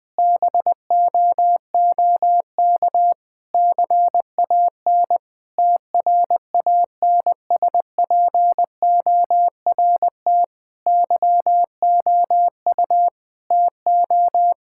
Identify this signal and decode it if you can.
Morse code, heard as a keyed tone, about 20 wpm, 700 Hz. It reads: BOOK CAN TRANSPORT YOU TO
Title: Difficult Sentences with Hints - 20wpm